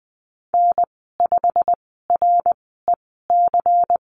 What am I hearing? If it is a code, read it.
Morse code, heard as a keyed tone, about 20 wpm, spaced as for 14 wpm, 700 Hz. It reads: N5REC
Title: 219 Lesson 35g - Number 0 - Call Signs 20wpm